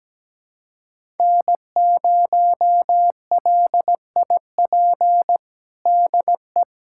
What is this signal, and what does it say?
Morse code, heard as a keyed tone, about 17 wpm, 700 Hz. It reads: N0LIP DE
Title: USA Callsign DE Callsign - 5 Characters - 17wpm